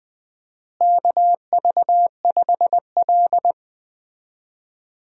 Morse code, heard as a keyed tone, about 20 wpm, 700 Hz. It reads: KV5L
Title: USA Call Signs - 4 Characters - Repeated 3x - 20wpm